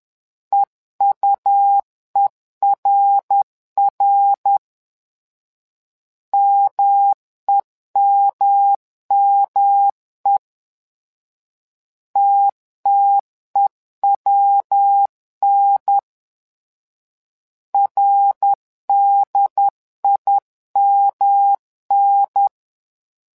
Morse code, heard as a keyed tone, about 11 wpm, 800 Hz. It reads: EUERR MEMME TTEWN RDIMN
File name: LEZIONI / 3 MWRO / 3 MWRO 10wpm 2148